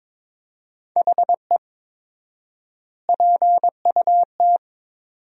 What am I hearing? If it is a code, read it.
Morse code, heard as a keyed tone, about 22 wpm, 700 Hz. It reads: HE PUT
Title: 4x Word-Spacing - Sentences from Top 400 Words - 22wpm